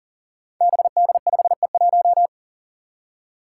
Morse code, heard as a keyed tone, about 40 wpm, 700 Hz. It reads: BD5E1
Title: Five Challenging Letter-Number Combinations 40wpm